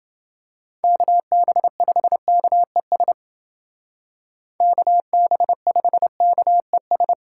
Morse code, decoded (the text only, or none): KB5KES KB5KES